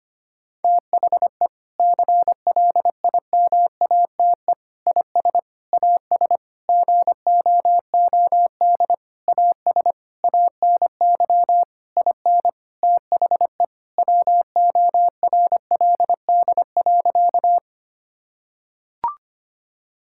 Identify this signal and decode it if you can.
Morse code, heard as a keyed tone, about 25 wpm, 700 Hz. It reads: THE CLIMATE IS AS GOOD AS ANY IN THE WORLD.
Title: Sentences from Top 1600 Words 25wpm